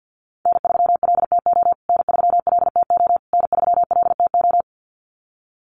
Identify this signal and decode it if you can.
Morse code, heard as a keyed tone, about 50 wpm, 700 Hz. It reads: N3LTO N3LTO N3LTO
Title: USA Call Signs Repeated 3x 50wpm